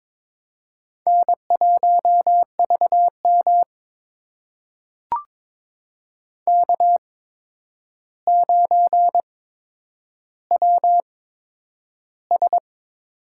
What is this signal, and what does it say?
Morse code, heard as a keyed tone, about 22 wpm, spaced as for 9 wpm, 700 Hz. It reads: N1VM K9WS
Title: USA Call Signs - 4 Characters - 8x Character Spacing - 22wpm